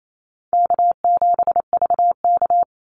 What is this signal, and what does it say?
Morse code, heard as a keyed tone, about 28 wpm, 700 Hz. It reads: K7VK
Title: NA SOTA Activator Callsigns 28wpm